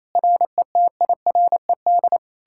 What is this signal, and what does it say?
Morse code, heard as a keyed tone, about 37 wpm, 700 Hz. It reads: RETIRED